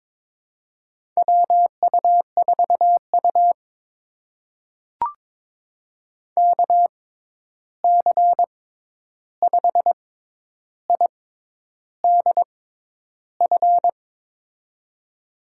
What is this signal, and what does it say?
Morse code, heard as a keyed tone, about 22 wpm, spaced as for 10 wpm, 700 Hz. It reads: WU4U KC5IDF
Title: USA Call Signs - 6x Character Spacing - 22wpm